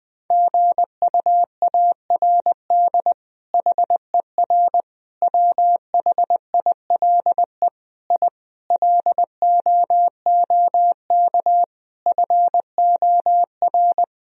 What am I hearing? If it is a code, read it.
Morse code, heard as a keyed tone, about 19 wpm, 700 Hz. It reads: GUARD HER WHILE I LOOK FOR
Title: Sentences from Top 1000 Words - Review Part 4 - 20wpm